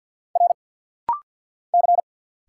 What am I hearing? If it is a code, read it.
Morse code, heard as a keyed tone, about 50 wpm, 700 Hz. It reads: R C
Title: Single Letters - Rapid-Fire 50wpm